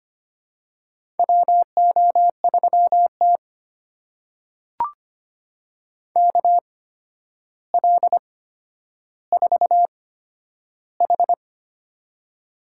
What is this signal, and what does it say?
Morse code, heard as a keyed tone, about 25 wpm, spaced as for 11 wpm, 700 Hz. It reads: WO3T KL4H
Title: USA Call Signs - 4 Characters - 8x Character Spacing - 25wpm